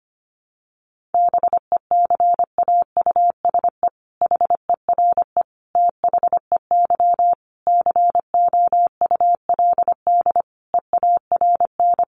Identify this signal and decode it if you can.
Morse code, heard as a keyed tone, about 25 wpm, 700 Hz. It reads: BECAUSE HERE THEY COULD EARN
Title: Sentences from Top 1500 Words 25wpm